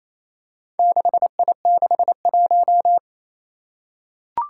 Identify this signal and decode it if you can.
Morse code, heard as a keyed tone, about 27 wpm, 700 Hz. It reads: 6I61
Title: Four Challenging Letter-Number Combinations 28wpm